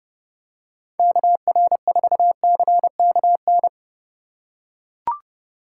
Morse code, heard as a keyed tone, about 30 wpm, 700 Hz. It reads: KR4CKN
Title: USA Call Signs - 2x Character Spacing - 30wpm